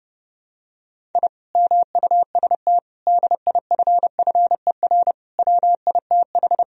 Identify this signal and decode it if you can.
Morse code, heard as a keyed tone, about 30 wpm, 700 Hz. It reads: I MUST DIFFER WITH
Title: Sentences from Top 1000 Words - Review Part 1 - 30wpm